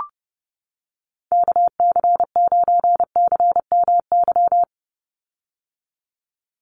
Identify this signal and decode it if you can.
Morse code, heard as a keyed tone, about 29 wpm, 700 Hz. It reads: KC9CMY
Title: USA Call Signs 30wpm